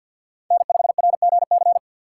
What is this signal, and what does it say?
Morse code, heard as a keyed tone, about 53 wpm, 700 Hz. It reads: N5RGC
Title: USA Call Signs - 6x Character Spacing - 50wpm